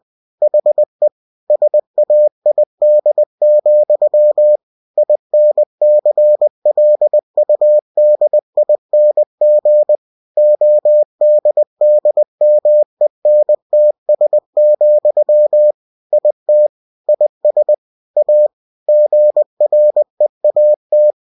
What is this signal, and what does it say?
Morse code, heard as a keyed tone, about 20 wpm, 600 Hz. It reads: HE SAID, INCLUDING ODDMENTS, IT IS A GREAT
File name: cw-books / winnie-the-pooh / winnie-20wpm-page-0107